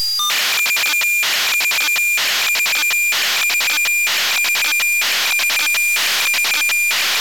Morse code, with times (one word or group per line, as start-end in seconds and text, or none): none